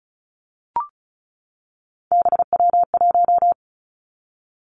Morse code, heard as a keyed tone, about 36 wpm, 700 Hz. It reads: BW1
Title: Three Challenging Letter-Number Combinations 35wpm